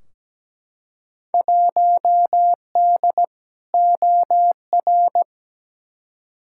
Morse code, heard as a keyed tone, about 17 wpm, 700 Hz. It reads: 1D OR
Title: Field Day Exchanges - 17wpm